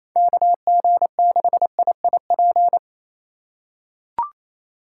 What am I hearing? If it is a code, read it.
Morse code, heard as a keyed tone, about 28 wpm, 700 Hz. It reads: KG6IIP
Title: K1USN SST QSO Practice - 28wpm